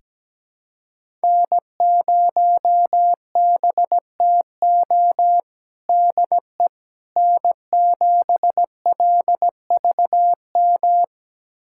Morse code, heard as a keyed tone, about 17 wpm, 700 Hz. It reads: N0BTO DE N7LVM
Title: USA Callsign DE Callsign - 5 Characters - 17wpm